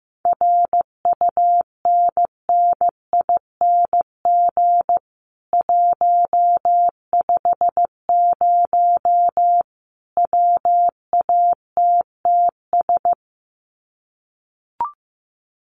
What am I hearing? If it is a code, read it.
Morse code, heard as a keyed tone, about 15 wpm, 700 Hz. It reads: RUNNING 150 WATTS
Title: Common QSO Phrases 15wpm